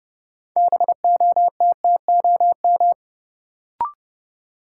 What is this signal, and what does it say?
Morse code, heard as a keyed tone, about 31 wpm, 700 Hz. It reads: BOTTOM
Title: Sets of 3 Words - New Word Drill - 30wpm